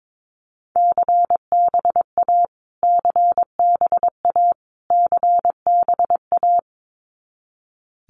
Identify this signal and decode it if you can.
Morse code, heard as a keyed tone, about 22 wpm, 700 Hz. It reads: CBA CBA CBA
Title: Abbreviations Repeated 3x - 22wpm